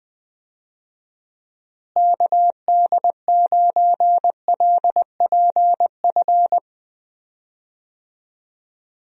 Morse code, heard as a keyed tone, about 21 wpm, 700 Hz. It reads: KD9LPF
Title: USA Call Signs - 6 Characters - Repeated 2x - 20wpm